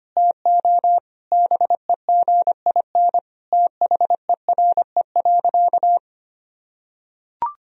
Morse code, heard as a keyed tone, about 25 wpm, 700 Hz. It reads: TO BEGIN THERE.
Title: Sentences from Top 1000 Words - Review Part 1 - 25wpm